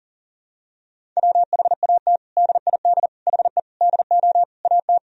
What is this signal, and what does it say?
Morse code, heard as a keyed tone, about 40 wpm, 700 Hz. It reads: WHAT DID HE DO AT
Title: Sentences from Top 1000 Words - Review Part 1 - 40wpm